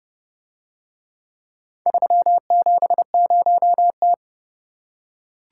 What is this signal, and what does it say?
Morse code, heard as a keyed tone, about 30 wpm, 700 Hz. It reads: 370T